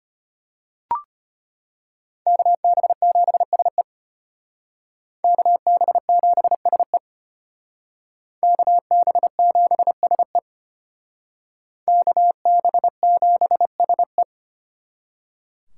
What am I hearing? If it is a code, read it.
Morse code, heard as a keyed tone, about 32 wpm, 700 Hz. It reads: KB7SE KB7SE KB7SE KB7SE